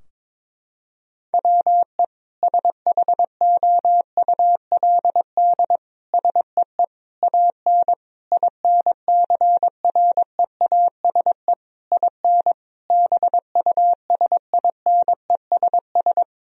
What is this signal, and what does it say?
Morse code, heard as a keyed tone, about 22 wpm, 700 Hz. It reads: WE SHOULD SEE AN INCREASE IN BUSINESS
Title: Sentences from Top 1000 Words - Review Part 4 - 22wpm